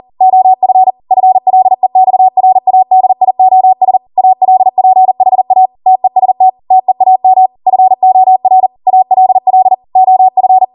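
Morse code, heard as a keyed tone, about 40 wpm, 750 Hz. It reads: OF FLEXRADIOS ALPHA TEST TEAM FOR ALL OF
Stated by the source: Code practice at 40 wpm.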